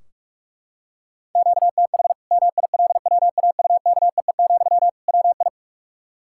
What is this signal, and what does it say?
Morse code, heard as a keyed tone, about 43 wpm, 700 Hz. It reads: QTH MILWAUKEE, WI
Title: Practice QSO - Exchange 1 - Beginning - 45wpm